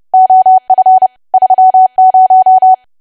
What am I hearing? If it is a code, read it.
Morse code, heard as a keyed tone, about 30 wpm, 750 Hz. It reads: OF 30
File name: cwa / W1AW-Level3 / 30WPM